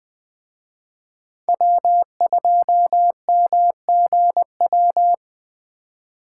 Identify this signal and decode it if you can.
Morse code, heard as a keyed tone, about 20 wpm, 700 Hz. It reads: W2MGW
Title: Speed Racing - USA 1x3 Call Signs 17wpm